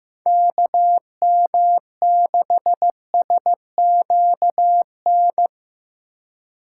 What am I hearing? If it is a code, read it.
Morse code, heard as a keyed tone, about 15 wpm, 700 Hz. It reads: KM6SQN
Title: USA Call Signs - 6 Characters - 2x Character Spacing - 15wpm